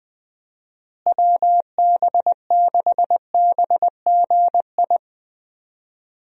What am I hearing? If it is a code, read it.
Morse code, heard as a keyed tone, about 21 wpm, 700 Hz. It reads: WB6BGI
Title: USA Call Signs - 6 Characters - Repeated 2x - 20wpm